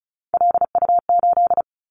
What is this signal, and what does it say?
Morse code, heard as a keyed tone, about 36 wpm, 700 Hz. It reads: LU8
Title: Callsign Prefixes for Top 100 DXCC - 35wpm